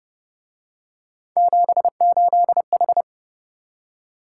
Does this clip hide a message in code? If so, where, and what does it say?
Yes: Morse code from 1.4 to 3.0 s, heard as a keyed tone, about 29 wpm, 700 Hz. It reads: 78H